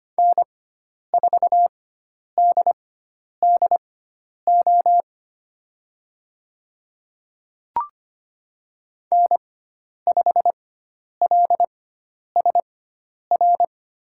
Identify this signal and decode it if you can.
Morse code, heard as a keyed tone, about 25 wpm, spaced as for 10 wpm, 700 Hz. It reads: N4DDO N5LSR
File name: Lesson 16g - Letter c - Call Signs 25wpm